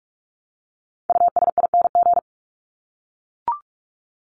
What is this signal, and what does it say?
Morse code, heard as a keyed tone, about 44 wpm, 700 Hz. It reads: USING
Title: ING Suffix - Top 100 Words with Base Words 45wpm